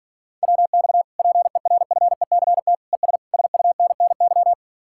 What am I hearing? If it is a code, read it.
Morse code, heard as a keyed tone, about 50 wpm, 700 Hz. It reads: WX PERFECT ES SUNNY